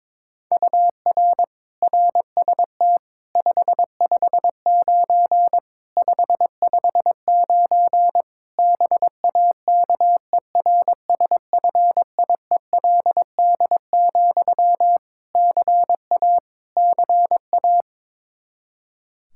Morse code, heard as a keyed tone, about 22 wpm, 700 Hz. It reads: UR RST 559 559 BAKERSFIELD, CA CA
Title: K1USN SST QSO Practice 22wpm